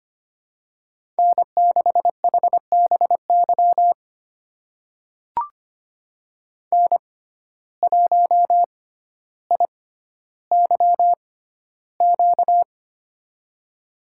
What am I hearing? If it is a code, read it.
Morse code, heard as a keyed tone, about 25 wpm, spaced as for 13 wpm, 700 Hz. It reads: N6HBY N1IYQ
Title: USA Call Signs - 6x Character Spacing - 25wpm